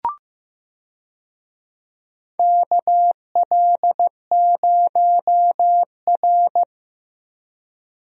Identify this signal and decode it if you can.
Morse code, heard as a keyed tone, about 15 wpm, 700 Hz. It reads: KL0R
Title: USA Call Signs - 4 Characters - Repeated 2x - 15wpm